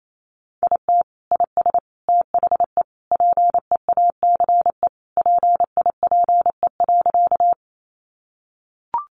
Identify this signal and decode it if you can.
Morse code, heard as a keyed tone, about 28 wpm, 700 Hz. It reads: IT IS THE PEACE PIPE.
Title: Sentences from Top 1500 Words 28wpm